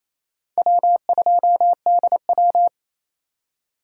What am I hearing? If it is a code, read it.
Morse code, heard as a keyed tone, about 35 wpm, 700 Hz. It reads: W2DW
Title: Speed Racing - USA 1x2 Call Signs 28wpm